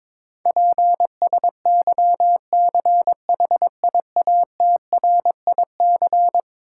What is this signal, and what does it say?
Morse code, heard as a keyed tone, about 22 wpm, 700 Hz. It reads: PSYCHIATRIC
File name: practice / 22wpm / Eleven Letter Words - 22wpm